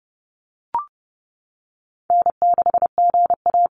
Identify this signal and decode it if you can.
Morse code, heard as a keyed tone, about 30 wpm, 700 Hz. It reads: N6GA
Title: Easy Call Signs - 30wpm